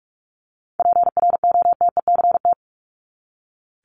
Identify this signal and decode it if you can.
Morse code, heard as a keyed tone, about 44 wpm, 700 Hz. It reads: PROTECT
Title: Top 751-1000 Words 45wpm